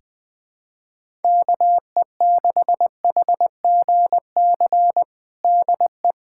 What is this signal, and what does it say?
Morse code, heard as a keyed tone, about 20 wpm, 700 Hz. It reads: KE6HGC DE
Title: USA Callsign DE Callsign - 6 Characters - 20wpm